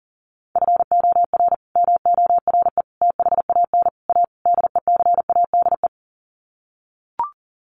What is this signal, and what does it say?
Morse code, heard as a keyed tone, about 40 wpm, 700 Hz. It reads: FOR MORE THAN A DECADE
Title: Sets of 5 Words 40wpm